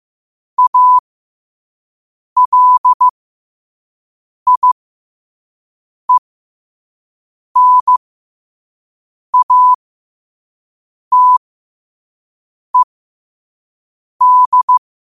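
Morse code, heard as a keyed tone, about 15 wpm, spaced as for 5 wpm, 1000 Hz. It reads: ALIENATED
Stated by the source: Morse code at 5 wpm.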